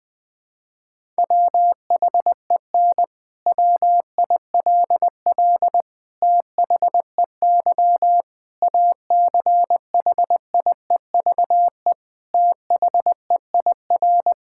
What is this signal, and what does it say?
Morse code, heard as a keyed tone, about 20 wpm, 700 Hz. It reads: WHEN WILL THEY ACHIEVE THEIR
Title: Sentences from Top 1000 Words - Review Part 6 - 20wpm